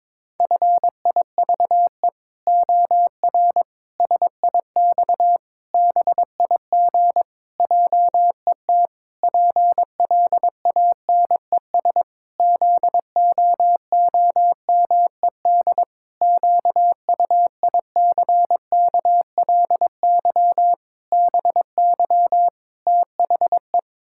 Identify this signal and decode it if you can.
Morse code, heard as a keyed tone, about 21 wpm, 700 Hz. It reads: FIVE OR SIX BIG JET PLANES ZOOMED QUICKLY BY THE
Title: Pangram Challenge v2 - 22wpm